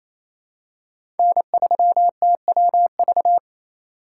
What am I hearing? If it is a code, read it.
Morse code, heard as a keyed tone, about 27 wpm, 700 Hz. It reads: N3TWV
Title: USA Call Signs - 5 Characters - 4x Character Spacing - 28wpm